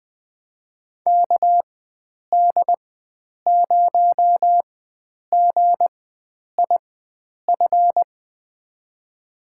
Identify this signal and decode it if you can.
Morse code, heard as a keyed tone, about 20 wpm, spaced as for 9 wpm, 700 Hz. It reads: KD0GIF
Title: USA Call Signs - 6 Characters - 4x Character Spacing - 20wpm